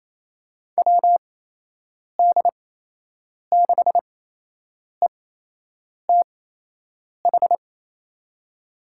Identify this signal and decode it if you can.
Morse code, heard as a keyed tone, about 28 wpm, spaced as for 8 wpm, 700 Hz. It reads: WD6ETH
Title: USA Call Signs - 6 Characters - 8x Character Spacing - 28wpm